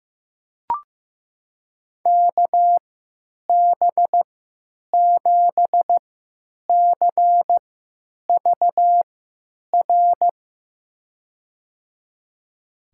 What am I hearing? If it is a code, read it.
Morse code, heard as a keyed tone, about 15 wpm, spaced as for 9 wpm, 700 Hz. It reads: KB7CVR
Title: USA Call Signs - 6 Characters - 3x Character Spacing - 15wpm